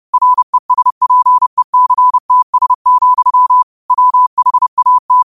Morse code, heard as a keyed tone, about 30 wpm, 1000 Hz. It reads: RESPECTS, WHAT
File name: qotdfiles / 30wpm